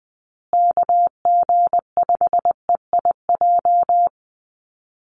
Morse code, heard as a keyed tone, about 20 wpm, 700 Hz. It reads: KG5EIJ
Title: USA Call Signs Repeated 3x 20wpm